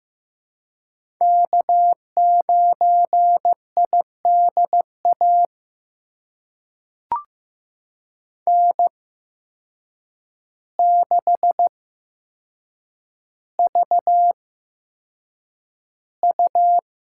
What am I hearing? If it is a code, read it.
Morse code, heard as a keyed tone, about 15 wpm, spaced as for 7 wpm, 700 Hz. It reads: K9IDA N6VU
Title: USA Call Signs - 5 Characters - 8x Character Spacing - 15wpm